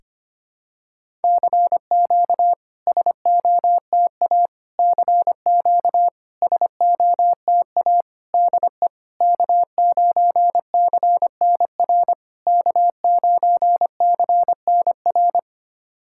Morse code, heard as a keyed tone, about 25 wpm, 690 Hz. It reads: CQ SOTA CQ SOTA DE K9CNR K9CNR
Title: SOTA QSO Practice 25wpm